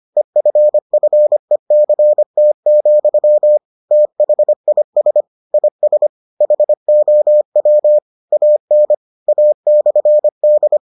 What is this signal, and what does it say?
Morse code, heard as a keyed tone, about 25 wpm, 600 Hz. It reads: EFFECT, THIS IS HOW AN A/D